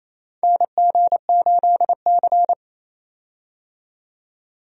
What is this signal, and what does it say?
Morse code, heard as a keyed tone, about 27 wpm, 700 Hz. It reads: NG8C
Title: USA Call Signs - 4 Characters - 28wpm